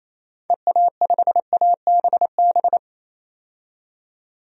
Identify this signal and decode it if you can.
Morse code, heard as a keyed tone, about 28 wpm, 700 Hz. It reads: EA5ABB
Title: DXCC-100 Call Signs - 6 Characters - 28wpm